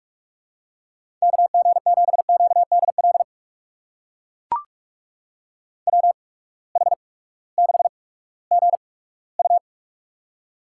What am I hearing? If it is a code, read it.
Morse code, heard as a keyed tone, about 43 wpm, spaced as for 20 wpm, 700 Hz. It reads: KG7QDL WH6GU
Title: USA Call Signs - 8x Character Spacing - 45wpm